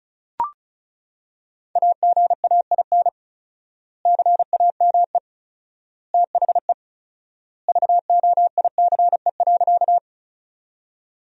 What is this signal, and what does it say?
Morse code, heard as a keyed tone, about 34 wpm, 700 Hz. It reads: AGAIN CAME THE VOICE.
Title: 4x Word-Spacing - Sentences from Top 500 Words - 35wpm